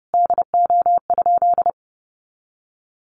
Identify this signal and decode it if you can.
Morse code, heard as a keyed tone, about 30 wpm, 700 Hz. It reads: DO?
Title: Sentences from Top 200 Words 30wpm